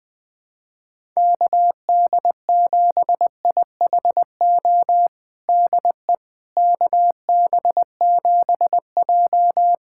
Morse code, heard as a keyed tone, about 20 wpm, 700 Hz. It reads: KD7IHO DE KB7J
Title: USA Callsign DE Callsign 20wpm